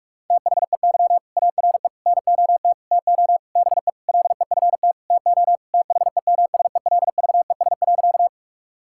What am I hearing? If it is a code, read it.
Morse code, heard as a keyed tone, about 42 wpm, 700 Hz. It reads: THEY ARE NOT TO BE LEFT TO THEMSELVES.
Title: Sentences from Top 1000 Words - Review Part 5 - 45wpm